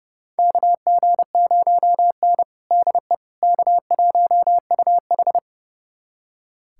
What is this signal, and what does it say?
Morse code, heard as a keyed tone, about 30 wpm, 700 Hz. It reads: KG0N DE K1UH
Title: USA Callsign DE Callsign - 4 Characters - 30wpm